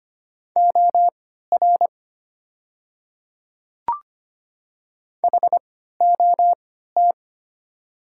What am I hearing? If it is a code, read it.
Morse code, heard as a keyed tone, about 25 wpm, spaced as for 14 wpm, 700 Hz. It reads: OR HOT
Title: 3x Character Spacing - Top 100 Words - 25wpm